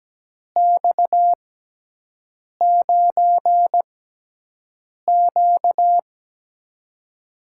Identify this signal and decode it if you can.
Morse code, heard as a keyed tone, about 17 wpm, spaced as for 6 wpm, 700 Hz. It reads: X9Q
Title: USA Call Signs - 6x Character Spacing - 17wpm